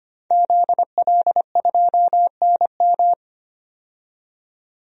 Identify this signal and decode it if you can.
Morse code, heard as a keyed tone, about 25 wpm, 700 Hz. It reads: ZL2NM
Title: DXCC-100 Call Signs - 5 Characters - 25wpm